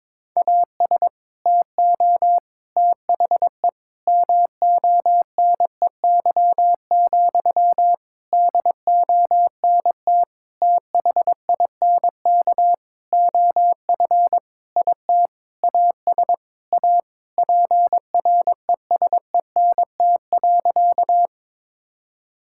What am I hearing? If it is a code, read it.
Morse code, heard as a keyed tone, about 22 wpm, 700 Hz. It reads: AS TO THE MONEY, DONT THINK OF IT AS A PRESENT.
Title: Sentences from Top 1000 Words - Review Part 5 - 22wpm